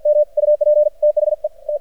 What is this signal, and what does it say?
Morse code, heard as a keyed tone, about 57 wpm, 600 Hz. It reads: M UP THE T
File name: archived / CRSnet